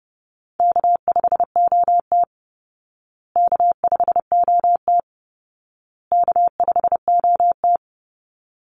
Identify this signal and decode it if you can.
Morse code, heard as a keyed tone, about 30 wpm, 700 Hz. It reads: K5OT K5OT K5OT
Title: Easy Call Signs Repeated 3x - 30wpm